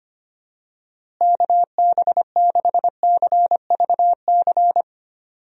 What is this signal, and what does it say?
Morse code, heard as a keyed tone, about 25 wpm, 700 Hz. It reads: KB6CVC
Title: USA Call Signs - 6 Characters - Repeated 2x - 25wpm